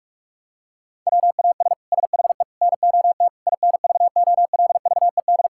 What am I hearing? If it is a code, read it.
Morse code, heard as a keyed tone, about 46 wpm, 700 Hz. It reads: WAS SHE NOT INVOLVED
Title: Sentences from Top 1000 Words - Review Part 5 - 45wpm